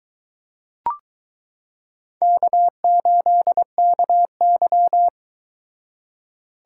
Morse code, heard as a keyed tone, about 22 wpm, 700 Hz. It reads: K8KY